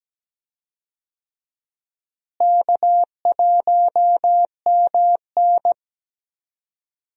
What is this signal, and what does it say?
Morse code, heard as a keyed tone, about 17 wpm, 700 Hz. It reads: K1MN
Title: USA Call Signs - 4 Characters - Repeated 3x - 17wpm